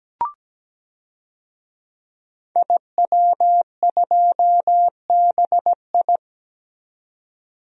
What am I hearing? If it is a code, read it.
Morse code, heard as a keyed tone, about 17 wpm, 700 Hz. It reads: IW2BI
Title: DXCC-100 Call Signs - 5 Characters - 17wpm